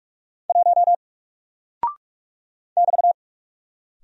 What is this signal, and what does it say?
Morse code, heard as a keyed tone, about 44 wpm, 700 Hz. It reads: 1 <BT>
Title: Single Character (Letter:Number:Punctuation:Prosign) Alt 45wpm